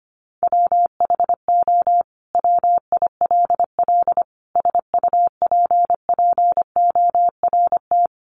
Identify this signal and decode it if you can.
Morse code, heard as a keyed tone, about 25 wpm, 700 Hz. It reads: WHO WILL SUPPORT
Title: Sentences from Top 800 Words 25wpm